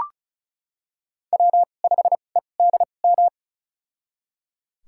Morse code, heard as a keyed tone, about 36 wpm, spaced as for 25 wpm, 690 Hz. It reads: W5EDM
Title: USA Call Signs - 5 Characters - 2x Character Spacing - 35wpm